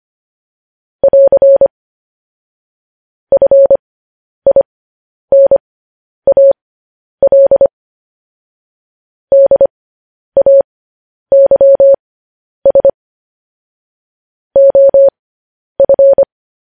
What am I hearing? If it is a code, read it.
Morse code, heard as a keyed tone, about 25 wpm, spaced as for 10 wpm, 550 Hz. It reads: <AR> FINAL DAYS OF